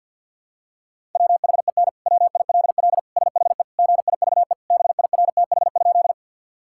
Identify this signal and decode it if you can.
Morse code, heard as a keyed tone, about 50 wpm, 700 Hz. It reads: WHEN WILL SHE GIVE BIRTH?
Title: Sentences from Top 1000 Words - Review Part 5 - 50wpm